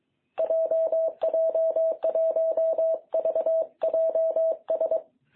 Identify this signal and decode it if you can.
Morse code, heard as a keyed tone, about 23 wpm, 660 Hz. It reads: JJ1VJS
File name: jh8cbh / 2021cw